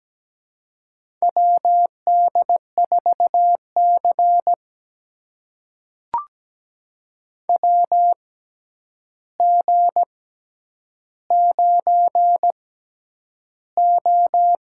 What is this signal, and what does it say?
Morse code, heard as a keyed tone, about 17 wpm, spaced as for 9 wpm, 700 Hz. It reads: WD4C WG9O
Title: USA Call Signs - 4 Characters - 6x Character Spacing - 17wpm